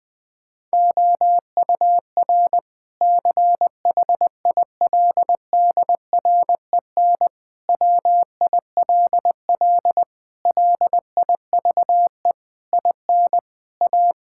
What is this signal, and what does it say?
Morse code, heard as a keyed tone, about 20 wpm, 700 Hz. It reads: OUR CHILDREN WILL LIVE IN A
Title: Sentences from Top 1000 Words - Review Part 5 - 20wpm